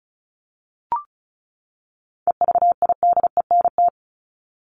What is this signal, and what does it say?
Morse code, heard as a keyed tone, about 34 wpm, 700 Hz. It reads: EVIDENT
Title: LY Suffix - Top 100 Words with Base Words 35wpm